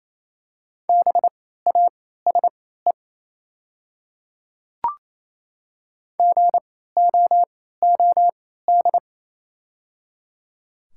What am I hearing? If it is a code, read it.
Morse code, heard as a keyed tone, about 28 wpm, spaced as for 16 wpm, 700 Hz. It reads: BASE GOOD
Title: 3x Character Spacing - Top 300 Words - 28wpm